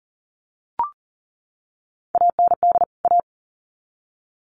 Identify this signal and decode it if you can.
Morse code, heard as a keyed tone, about 40 wpm, 700 Hz. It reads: AND A
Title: Sets of 2 Words 40wpm